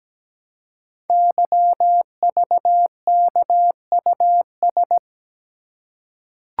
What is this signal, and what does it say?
Morse code, heard as a keyed tone, about 17 wpm, 700 Hz. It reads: YVKUS
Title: Code Groups - 5 Letters - 17wpm